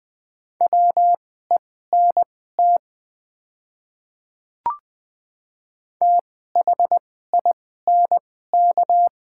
Morse code, heard as a keyed tone, about 20 wpm, spaced as for 15 wpm, 700 Hz. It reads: WENT THINK
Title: 2x Character Spacing - Top 200 Words - 20wpm